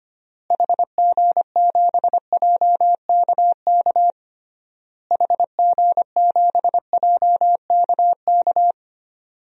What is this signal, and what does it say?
Morse code, heard as a keyed tone, about 25 wpm, 700 Hz. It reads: HG7JKK HG7JKK
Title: DXCC-100 Call Signs - 6 Characters - Repeated 3x - 25wpm